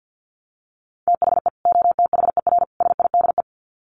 Morse code, heard as a keyed tone, about 50 wpm, 700 Hz. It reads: THE OTHER SIDE
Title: Sets of 3 Words 50wpm